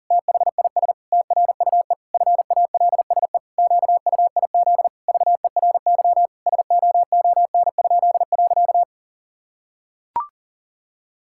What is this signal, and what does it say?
Morse code, heard as a keyed tone, about 40 wpm, 700 Hz. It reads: THIS TRUE FALSE QUIZ VERY SOON?.
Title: Pangram Challenge v2 - 40wpm